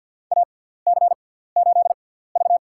Morse code, heard as a keyed tone, about 46 wpm, spaced as for 17 wpm, 700 Hz. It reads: AC8V
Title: USA Call Signs - 4 Characters - 6x Character Spacing - 50wpm